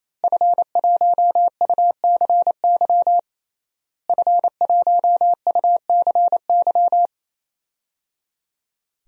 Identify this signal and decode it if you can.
Morse code, heard as a keyed tone, about 28 wpm, 700 Hz. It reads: F1UCY F1UCY
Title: DXCC-100 Call Signs - 5 Characters - Repeated 3x - 28wpm